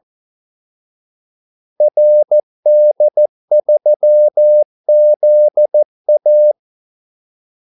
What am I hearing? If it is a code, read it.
Morse code, heard as a keyed tone, about 14 wpm, 600 Hz. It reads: RD3ZA
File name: Callsigns-14wpm